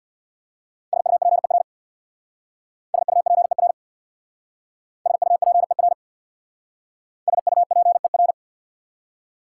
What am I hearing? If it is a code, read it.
Morse code, heard as a keyed tone, about 61 wpm, 690 Hz. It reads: SUPER SUPER SUPER SUPER
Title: Speed-Racing Prefixes and Suffixes 50wpm